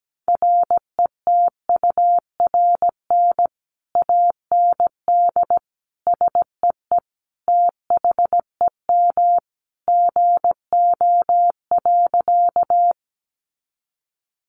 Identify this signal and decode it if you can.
Morse code, heard as a keyed tone, about 17 wpm, 700 Hz. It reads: RETURN AND SEE THEM GO.
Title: Sentences from Top 800 Words 17wpm